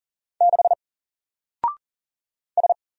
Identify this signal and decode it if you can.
Morse code, heard as a keyed tone, about 40 wpm, 700 Hz. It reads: 6 S
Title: Single Letter-Number 40wpm